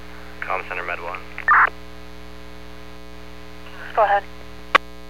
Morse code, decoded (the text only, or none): none